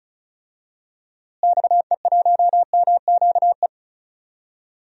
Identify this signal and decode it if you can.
Morse code, heard as a keyed tone, about 34 wpm, 700 Hz. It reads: XE1MQE
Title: DXCC-100 Call Signs - 6 Characters - Repeated 3x - 35wpm